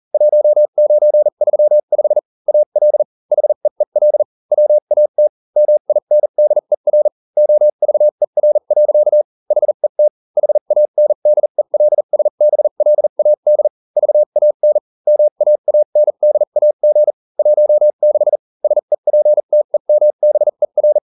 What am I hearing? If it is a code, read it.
Morse code, heard as a keyed tone, about 40 wpm, spaced as for 35 wpm, 600 Hz. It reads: 1935 AL HEEL WAT MINDER OVER. HET HANDELSBLAD VAN MAANDAG 16 SEPTEMBER